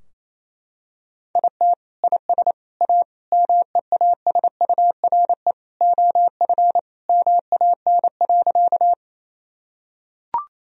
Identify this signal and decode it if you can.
Morse code, heard as a keyed tone, about 28 wpm, 700 Hz. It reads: IT IS A MEASURE OF MAN.
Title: Sentences from Top 1000 Words - Review Part 2 - 28wpm